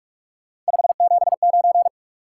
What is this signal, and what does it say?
Morse code, heard as a keyed tone, about 44 wpm, 700 Hz. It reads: 579
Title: Top 180 QSO Elements 45wpm